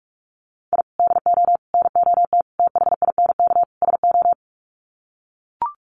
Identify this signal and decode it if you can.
Morse code, heard as a keyed tone, about 44 wpm, 700 Hz. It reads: I DO NOT THINK SO
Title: Sets of 5 Words 45wpm